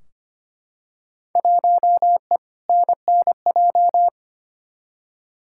Morse code, heard as a keyed tone, about 25 wpm, 700 Hz. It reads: 1E NNJ
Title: Field Day Exchanges - 25wpm